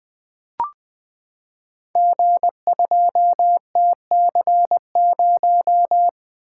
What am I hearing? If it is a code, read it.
Morse code, heard as a keyed tone, about 20 wpm, 700 Hz. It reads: G2TC0
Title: Code Groups - 5 Letter:Numbers - 20wpm